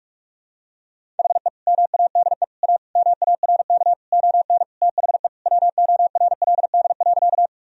Morse code, heard as a keyed tone, about 46 wpm, 700 Hz. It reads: HE MADE A MARK ON THE WORLD.
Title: Sentences from Top 1000 Words - Review Part 1 - 45wpm